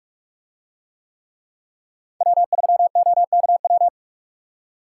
Morse code, heard as a keyed tone, about 44 wpm, 700 Hz. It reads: W3OKW